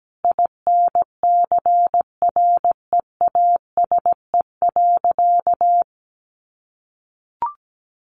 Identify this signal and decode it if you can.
Morse code, heard as a keyed tone, about 17 wpm, 700 Hz. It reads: INCREASE.
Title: Sentences from Top 1400 Words 17wpm